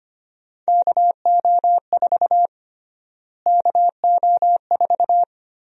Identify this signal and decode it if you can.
Morse code, heard as a keyed tone, about 25 wpm, 700 Hz. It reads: KO4 KO4
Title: USA Call Sign Prefixes - Repeated 2x - 25wpm